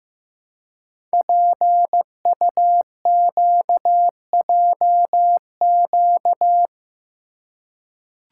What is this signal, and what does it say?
Morse code, heard as a keyed tone, about 15 wpm, 700 Hz. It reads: PUQJQ
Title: Code Groups - 5 Letters - 15wpm